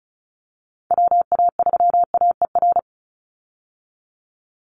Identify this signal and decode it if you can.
Morse code, heard as a keyed tone, about 34 wpm, 700 Hz. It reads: WA3AER
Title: Call Signs - 35wpm